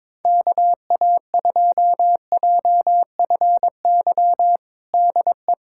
Morse code, heard as a keyed tone, about 22 wpm, 700 Hz. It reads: KA2JFY DE
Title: USA Callsign DE Callsign - 6 Characters - 22wpm